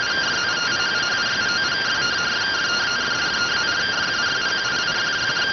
Two-tone teleprinter signal, about 170 Hz shift.